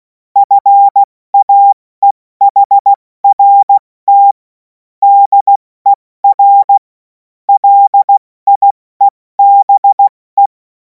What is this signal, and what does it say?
Morse code, heard as a keyed tone, about 16 wpm, 800 Hz. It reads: FAEHRT DER LIEBE
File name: Morsekurs / Witze / Witz 24. März 2026 70 F80